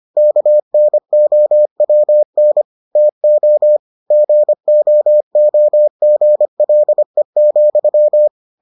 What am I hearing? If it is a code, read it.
Morse code, heard as a keyed tone, about 25 wpm, 600 Hz. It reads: KNOWN TO GOOGLE,